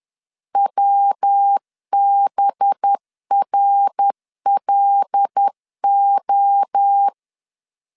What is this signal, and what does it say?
Morse code, heard as a keyed tone, about 10 wpm, 800 Hz. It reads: WBRLO
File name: LEZIONI / 6 PKX0 / 6 PKX0 10wpm 2707